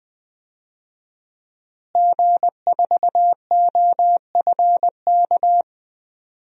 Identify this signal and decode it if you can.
Morse code, heard as a keyed tone, about 20 wpm, 700 Hz. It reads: G4OFK
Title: DXCC-100 Call Signs - 5 Characters - Repeated 3x - 20wpm